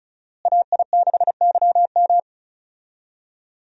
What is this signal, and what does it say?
Morse code, heard as a keyed tone, about 36 wpm, 700 Hz. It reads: AI6YM
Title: USA Call Signs - 5 Characters - Repeated 3x - 35wpm